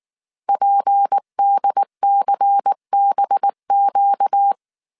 Morse code, heard as a keyed tone, about 19 wpm, 800 Hz. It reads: PDCBQ